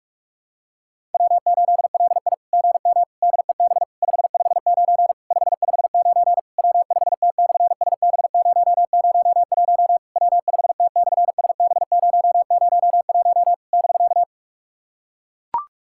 Morse code, heard as a keyed tone, about 43 wpm, 690 Hz. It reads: W8LI GM DEB 559 559 W5T/SB001 W5T/SB001 <BK>
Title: SOTA QSO Practice 45wpm